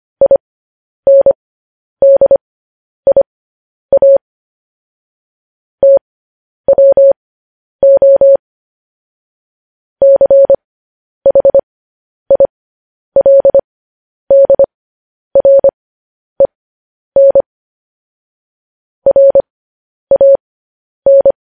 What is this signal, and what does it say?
Morse code, heard as a keyed tone, about 25 wpm, spaced as for 10 wpm, 550 Hz. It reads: INDIA TWO CHILDREN RAN